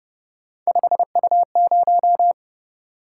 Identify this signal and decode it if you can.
Morse code, heard as a keyed tone, about 28 wpm, 700 Hz. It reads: 5U0